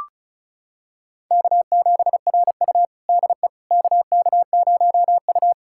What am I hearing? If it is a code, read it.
Morse code, heard as a keyed tone, about 34 wpm, 700 Hz. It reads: K7RU DE KK0U
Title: USA Callsign DE Callsign - 4 Characters - 35wpm